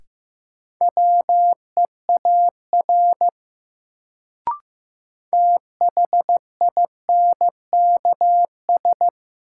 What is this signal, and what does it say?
Morse code, heard as a keyed tone, about 15 wpm, 700 Hz. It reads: WEAR THINKS
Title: Sentences from Top 1000 Words - Review Part 6 - New Word Drill - 15wpm